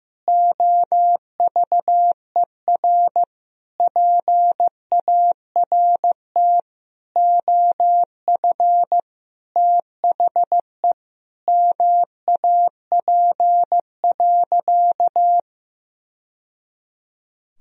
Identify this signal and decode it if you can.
Morse code, heard as a keyed tone, about 15 wpm, 700 Hz. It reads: OVER PART OF THE MAP.
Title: Sentences from Top 1000 Words - Review Part 2 - 15wpm